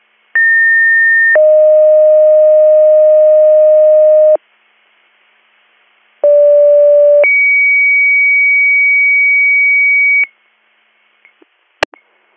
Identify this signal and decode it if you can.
Two-tone page, heard as a one-second tone, then a three-second tone, about 1800 then 620 Hz, about 580 then 2250 Hz. It is heard from 0.4 to 10.2 s.